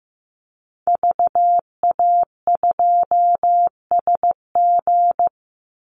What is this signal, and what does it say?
Morse code, heard as a keyed tone, about 15 wpm, 700 Hz. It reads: VA2SG
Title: Call Signs Repeated 2x - 15wpm